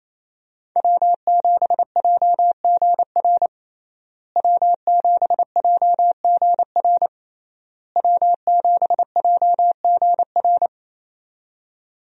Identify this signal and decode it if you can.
Morse code, heard as a keyed tone, about 28 wpm, 700 Hz. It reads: W7JGR W7JGR W7JGR
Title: USA Call Signs - 5 Characters - Repeated 3x - 28wpm